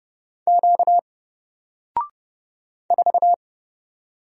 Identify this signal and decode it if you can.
Morse code, heard as a keyed tone, about 29 wpm, 700 Hz. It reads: Q 4
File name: Single-Letter-Number-30wpm